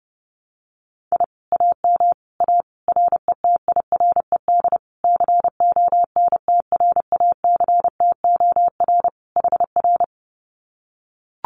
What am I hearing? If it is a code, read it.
Morse code, heard as a keyed tone, about 30 wpm, 700 Hz. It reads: I AM A RETIRED CONTRACTOR HR
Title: Common QSO Phrases Encore 30wpm